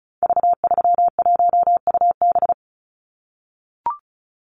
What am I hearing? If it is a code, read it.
Morse code, heard as a keyed tone, about 34 wpm, 700 Hz. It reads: V31UB
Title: Call Signs - 35wpm